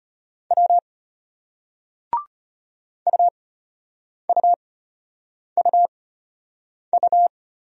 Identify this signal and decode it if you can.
Morse code, heard as a keyed tone, about 33 wpm, spaced as for 8 wpm, 700 Hz. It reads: W UUUU